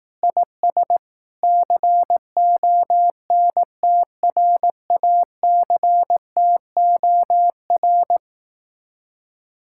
Morse code, heard as a keyed tone, about 18 wpm, 700 Hz. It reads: IS CONTRACTOR
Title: Speed-Racing Common QSO Phrases Encore - 15wpm